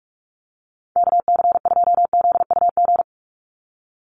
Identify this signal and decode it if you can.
Morse code, heard as a keyed tone, about 44 wpm, 700 Hz. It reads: KC2ZUG
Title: USA Call Signs 45wpm